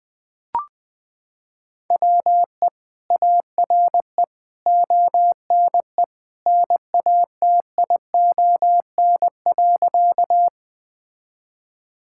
Morse code, heard as a keyed tone, about 21 wpm, 700 Hz. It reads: WE ARE ONE NATION.
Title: Sentences from Top 1000 Words - Review Part 6 - 20wpm